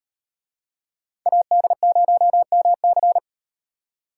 Morse code, heard as a keyed tone, about 38 wpm, 700 Hz. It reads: AD0MC